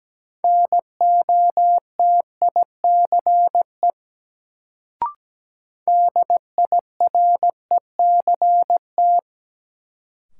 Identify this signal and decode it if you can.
Morse code, heard as a keyed tone, about 17 wpm, 700 Hz. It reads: NOTICE DIRECT
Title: Sentences from Top 400 Words - New Word Drill - 17wpm